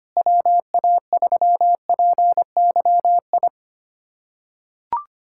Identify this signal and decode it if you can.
Morse code, heard as a keyed tone, about 25 wpm, 700 Hz. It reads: WA3PYI
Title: USA Callsign DE Callsign - 6 Characters - 25wpm